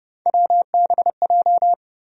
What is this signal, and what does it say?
Morse code, heard as a keyed tone, about 29 wpm, 700 Hz. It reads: WBJ
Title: USA Call Sign Postfixes - 3 Letters 30wpm